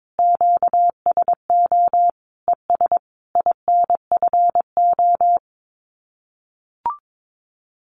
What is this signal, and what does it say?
Morse code, heard as a keyed tone, about 22 wpm, 700 Hz. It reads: QSO ES INFO
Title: Common QSO Phrases Encore 22wpm